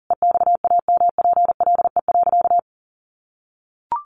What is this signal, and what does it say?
Morse code, heard as a keyed tone, about 40 wpm, 700 Hz. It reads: EXAMPLE.
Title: Sentences from Top 1500 Words 40wpm